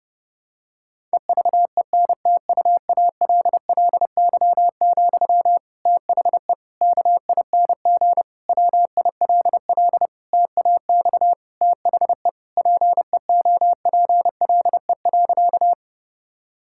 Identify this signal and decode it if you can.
Morse code, heard as a keyed tone, about 30 wpm, 700 Hz. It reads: EVENTUALLY, THE KING WILL TAX THE PEOPLE.
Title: Sentences from Top 1000 Words - Review Part 6 - 30wpm